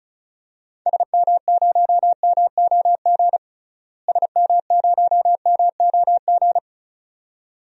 Morse code, heard as a keyed tone, about 36 wpm, 700 Hz. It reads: SM0MOG SM0MOG
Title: DXCC-100 Call Signs - 6 Characters - Repeated 2x - 35wpm